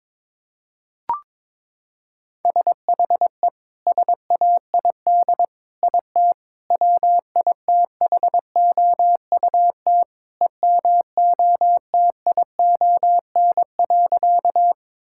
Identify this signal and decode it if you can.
Morse code, heard as a keyed tone, about 21 wpm, 700 Hz. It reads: SHE SAID IT WITHOUT EMOTION.
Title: Sentences from Top 1600 Words 22wpm